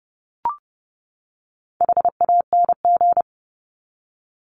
Morse code, heard as a keyed tone, about 30 wpm, 700 Hz. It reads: HANG
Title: Top 1501-1750 Words 30wpm